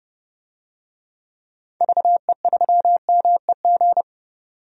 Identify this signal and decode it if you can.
Morse code, heard as a keyed tone, about 30 wpm, 700 Hz. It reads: VE3MEG
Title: DXCC-100 Call Signs - 6 Characters - 30wpm